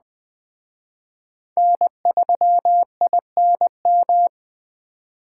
Morse code, heard as a keyed tone, about 20 wpm, 700 Hz. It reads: N3INM
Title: Speed Racing - USA 1x3 Call Signs 17wpm